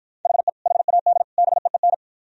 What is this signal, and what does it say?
Morse code, heard as a keyed tone, about 52 wpm, 700 Hz. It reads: HE HAD BEEN